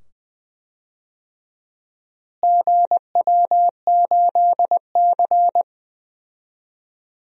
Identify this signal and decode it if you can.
Morse code, heard as a keyed tone, about 20 wpm, 700 Hz. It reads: GW8C
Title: DXCC-100 Call Signs - 4 Characters - Repeated 2x - 20wpm